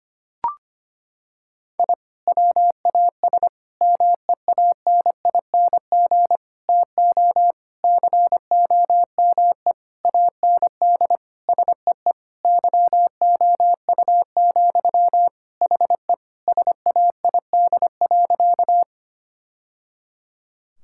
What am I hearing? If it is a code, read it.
Morse code, heard as a keyed tone, about 25 wpm, 700 Hz. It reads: I WAS MEANING TO COME AND SEE YOU, HE SAID.
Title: Sentences from Top 1000 Words - Review Part 6 - 25wpm